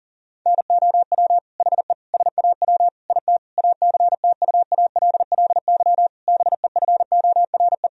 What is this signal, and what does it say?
Morse code, heard as a keyed tone, about 40 wpm, 700 Hz. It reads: NOW HE SAW IT ACTUALLY BEFORE
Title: Sentences from Top 1000 Words - Review Part 4 - 40wpm